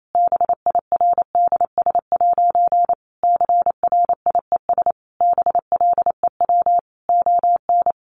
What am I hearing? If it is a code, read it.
Morse code, heard as a keyed tone, about 28 wpm, 700 Hz. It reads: BIRDS' CRIES BLEW ON
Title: A Sound Of Thunder 28wpm